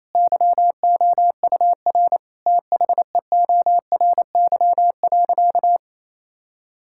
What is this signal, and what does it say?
Morse code, heard as a keyed tone, about 27 wpm, 700 Hz. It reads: YOUR THEORY.
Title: Sentences from Top 1000 Words - Review Part 5 - 28wpm